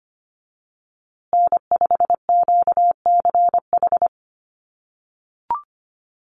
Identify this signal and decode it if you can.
Morse code, heard as a keyed tone, about 25 wpm, 700 Hz. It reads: N5QCH
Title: USA Call Signs Repeated 3x 25wpm